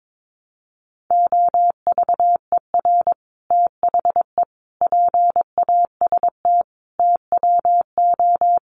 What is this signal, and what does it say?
Morse code, heard as a keyed tone, about 22 wpm, 700 Hz. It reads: OVER THE PAST TWO
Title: Sets of 5 Words Encore2 22wpm